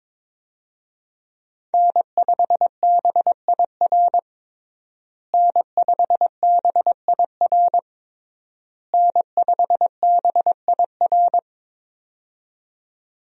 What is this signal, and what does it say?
Morse code, heard as a keyed tone, about 22 wpm, 700 Hz. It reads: N5BIR N5BIR N5BIR
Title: USA Call Signs - 5 Characters - Repeated 3x - 22wpm